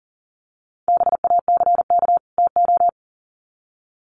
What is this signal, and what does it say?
Morse code, heard as a keyed tone, about 40 wpm, 700 Hz. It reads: BACK TO
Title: Sets of 2 Words 40wpm